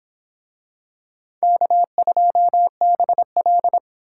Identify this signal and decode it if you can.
Morse code, heard as a keyed tone, about 25 wpm, 700 Hz. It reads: K2BL